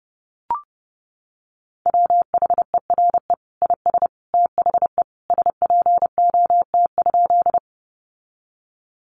Morse code, heard as a keyed tone, about 30 wpm, 700 Hz. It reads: WHERE IS THE SPOT?
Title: Sentences from Top 1100 Words 30wpm